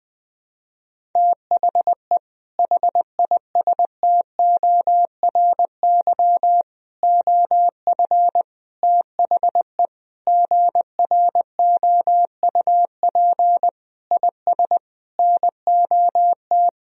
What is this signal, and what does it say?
Morse code, heard as a keyed tone, about 20 wpm, 700 Hz. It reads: THE HISTORY OF THE GROUP IS NOT
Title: Sentences from Top 1000 Words - Review Part 1 - 20wpm